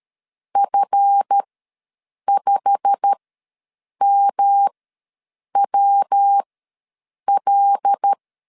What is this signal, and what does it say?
Morse code, heard as a keyed tone, about 13 wpm, spaced as for 7 wpm, 800 Hz. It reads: F5MWL
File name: LEZIONI / 9 28 ws 12wpm 3354